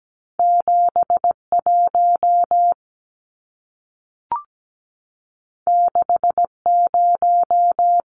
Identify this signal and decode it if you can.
Morse code, heard as a keyed tone, about 17 wpm, 700 Hz. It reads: 71 60
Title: Numbers - 2-Digit 17wpm